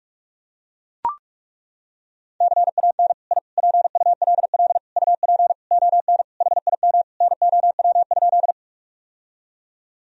Morse code, heard as a keyed tone, about 41 wpm, 700 Hz. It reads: CAN I PULL UP ON HIM NOW?
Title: Sentences from Top 1000 Words - Review Part 2 - 45wpm